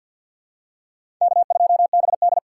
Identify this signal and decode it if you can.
Morse code, heard as a keyed tone, about 53 wpm, 700 Hz. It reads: K2BD
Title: USA Call Signs - 4 Characters - 6x Character Spacing - 50wpm